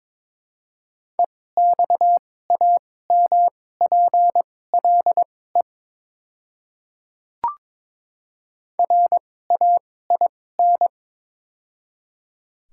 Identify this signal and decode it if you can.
Morse code, heard as a keyed tone, about 22 wpm, spaced as for 16 wpm, 700 Hz. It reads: EXAMPLE RAIN
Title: 2x Character Spacing - Top 300 Words - 22wpm